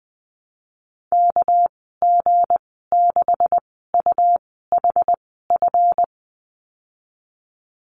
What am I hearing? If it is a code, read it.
Morse code, heard as a keyed tone, about 19 wpm, spaced as for 14 wpm, 700 Hz. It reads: KG6UHF
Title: 207 Lesson 33g - Letter j - Call Signs 20wpm